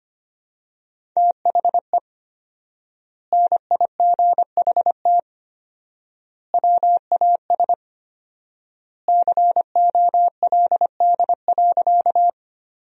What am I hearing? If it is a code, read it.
Morse code, heard as a keyed tone, about 25 wpm, 700 Hz. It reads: THE NIGHT WAS COLD.
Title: 4x Word-Spacing - Sentences from Top 400 Words - 25wpm